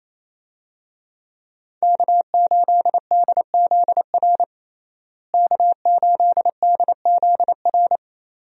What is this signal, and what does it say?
Morse code, heard as a keyed tone, about 28 wpm, 700 Hz. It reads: K8DZR K8DZR
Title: USA Call Signs - 5 Characters - Repeated 2x - 28wpm